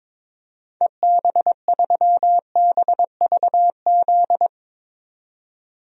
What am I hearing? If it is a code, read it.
Morse code, heard as a keyed tone, about 21 wpm, 700 Hz. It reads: EB3BVZ
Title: DXCC-100 Call Signs - 6 Characters - 22wpm